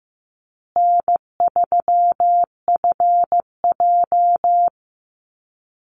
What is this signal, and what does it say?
Morse code, heard as a keyed tone, about 15 wpm, 700 Hz. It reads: N3FJ
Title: USA Call Signs Repeated 3x 15wpm